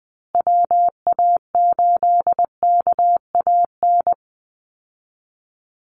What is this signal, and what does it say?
Morse code, heard as a keyed tone, about 20 wpm, 700 Hz. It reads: WA8KAN
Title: Call Signs Repeated 3x - 20wpm